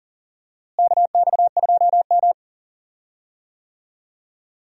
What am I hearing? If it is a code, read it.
Morse code, heard as a keyed tone, about 37 wpm, 700 Hz. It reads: KX2M
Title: USA Call Signs - 4 Characters - Repeated 3x - 40wpm